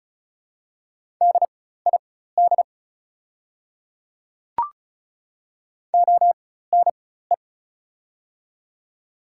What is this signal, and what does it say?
Morse code, heard as a keyed tone, about 34 wpm, spaced as for 16 wpm, 700 Hz. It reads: DID ONE
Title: 4x Character Spacing - Top 100 Words - 35wpm